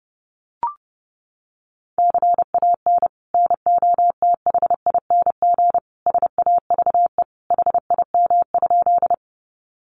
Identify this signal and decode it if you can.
Morse code, heard as a keyed tone, about 30 wpm, 700 Hz. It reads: CAN NOTHING SAVE HIM?
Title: Sentences from Top 1100 Words 30wpm